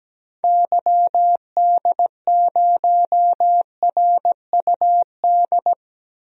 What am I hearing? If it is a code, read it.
Morse code, heard as a keyed tone, about 17 wpm, 700 Hz. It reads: YD0RUD
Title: DXCC-100 Call Signs - 6 Characters - Repeated 2x - 17wpm